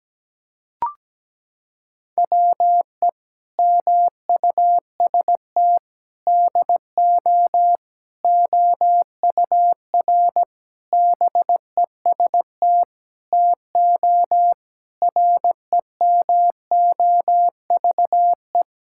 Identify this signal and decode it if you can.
Morse code, heard as a keyed tone, about 17 wpm, 700 Hz. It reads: WE MUST DO OUR BEST TO REMOVE
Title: Sentences from Top 1000 Words - Review Part 5 - 17wpm